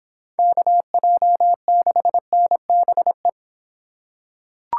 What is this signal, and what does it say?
Morse code, heard as a keyed tone, about 25 wpm, 700 Hz. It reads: KJ6NBE